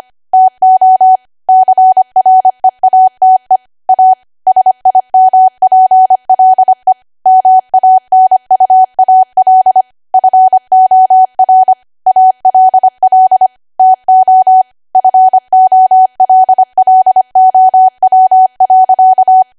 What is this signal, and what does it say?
Morse code, heard as a keyed tone, about 25 wpm, 750 Hz. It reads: TO CREATE A SIMPLE MANUAL FOR ALL TO FOLLOW.
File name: Archive / 25 WPM / 25WPM